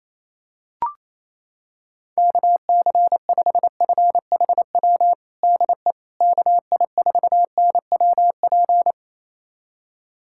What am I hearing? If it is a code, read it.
Morse code, heard as a keyed tone, about 28 wpm, 700 Hz. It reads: KC5FHW DE KI4NWP
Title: USA Callsign DE Callsign 28wpm